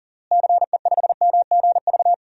Morse code, heard as a keyed tone, about 40 wpm, 700 Hz. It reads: CE5MGV
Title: DXCC-100 Call Signs - 6 Characters - Repeated 3x - 40wpm